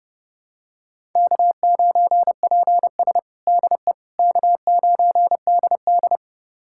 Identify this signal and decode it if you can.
Morse code, heard as a keyed tone, about 32 wpm, 700 Hz. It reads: K9PS DE K9DD
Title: USA Callsign DE Callsign - 4 Characters - 30wpm